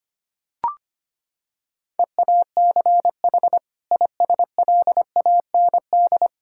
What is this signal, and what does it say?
Morse code, heard as a keyed tone, about 25 wpm, 700 Hz. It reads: EACH ISLAND
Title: Sentences from Top 1000 Words - Review Part 6 - 25wpm